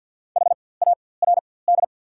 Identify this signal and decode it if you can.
Morse code, heard as a keyed tone, about 50 wpm, spaced as for 23 wpm, 690 Hz. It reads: HARD
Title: 4x Character Spacing - Top 300 Words - 50wpm